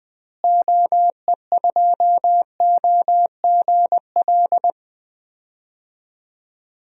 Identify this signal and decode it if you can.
Morse code, heard as a keyed tone, about 20 wpm, 700 Hz. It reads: OE2OGL
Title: DXCC-100 Call Signs - 6 Characters - Repeated 3x - 20wpm